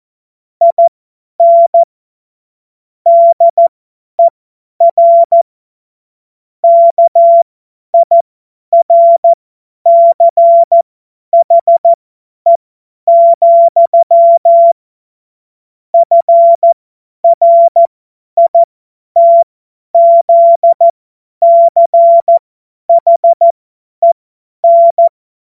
Morse code, heard as a keyed tone, about 14 wpm, spaced as for 10 wpm, 680 Hz. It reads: IN DER KIRCHE, FRITZCHEN
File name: Morsekurs / Witze / Witz 12. Oktober 2021, 50 F70